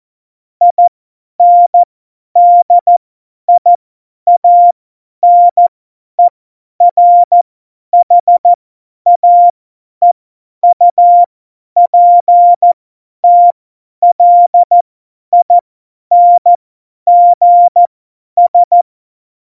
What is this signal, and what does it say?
Morse code, heard as a keyed tone, about 14 wpm, spaced as for 10 wpm, 700 Hz. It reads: INDIANERHAEUPTLINGS
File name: Morsekurs / Witze / Witz 21 Juli 2020, 50 F70